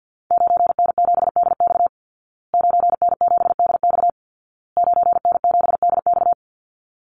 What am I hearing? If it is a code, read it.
Morse code, heard as a keyed tone, about 50 wpm, 700 Hz. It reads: 9N7DX 9N7DX 9N7DX
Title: Call Signs Repeated 2x - 50wpm